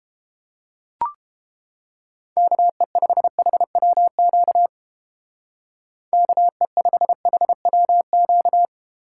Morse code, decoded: KE5HWQ KE5HWQ

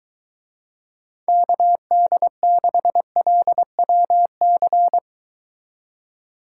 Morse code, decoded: KD6LWC

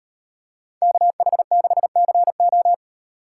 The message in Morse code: KH6CO

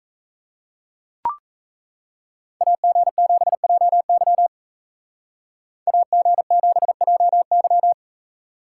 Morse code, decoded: AG7JY AG7JY

